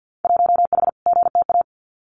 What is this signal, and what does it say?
Morse code, heard as a keyed tone, about 50 wpm, 700 Hz. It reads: 1H GTA
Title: Winter Field Day Exchanges 50wpm